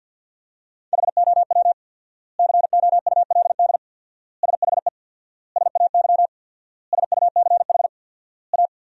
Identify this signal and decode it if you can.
Morse code, heard as a keyed tone, about 50 wpm, 700 Hz. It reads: HOW COULD SHE SAY SUCH A
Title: 4x Word-Spacing - Sentences from Top 200 Words - 50wpm